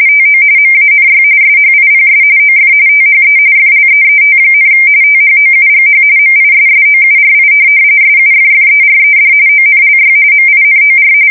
RTTY, about 45.45 baud. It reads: TO MANY SHIPS OF THE FLEET, THOUGH MISSOURI SUFFERED ONLY MINOR
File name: fsk170 / USS-Missouri